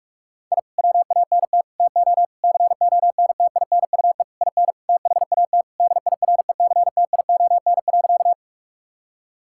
Morse code, heard as a keyed tone, about 46 wpm, 700 Hz. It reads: I WANT TO CONTINUE IN THAT DIRECTION.
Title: Sentences from Top 1000 Words - Review Part 6 - 45wpm